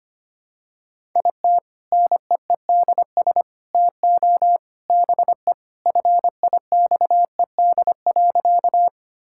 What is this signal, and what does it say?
Morse code, heard as a keyed tone, about 25 wpm, 700 Hz. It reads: IT NEEDS TO BE FIXED.
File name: practice / 25wpm / Sentences with 3-Word Phrases - 25wpm